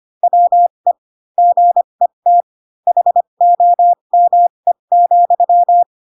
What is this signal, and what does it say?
Morse code, heard as a keyed tone, about 25 wpm, spaced as for 22 wpm, 700 Hz. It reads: WE GET HOME,
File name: Pinocchio / Pinocchio ch3 18 25 ik1hge